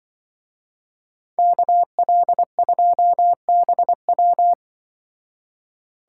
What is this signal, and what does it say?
Morse code, heard as a keyed tone, about 25 wpm, 700 Hz. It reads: KL2BW
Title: Speed Racing - USA 2x2 Call Signs 20wpm